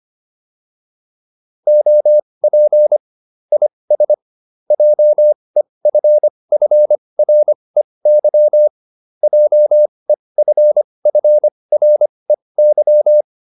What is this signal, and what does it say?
Morse code, heard as a keyed tone, about 25 wpm, spaced as for 20 wpm, 600 Hz. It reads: OP IS JEFFREY JEFFREY